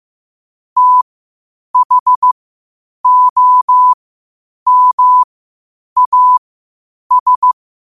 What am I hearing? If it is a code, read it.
Morse code, heard as a keyed tone, about 15 wpm, spaced as for 8 wpm, 1000 Hz. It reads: THOMAS
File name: qotdfiles / 8wpm